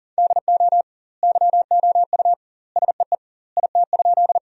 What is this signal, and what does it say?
Morse code, heard as a keyed tone, about 37 wpm, 700 Hz. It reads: DO YOU SEE IT?
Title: 2x Word-Spacing - Sentences from Top 100 Words - 40wpm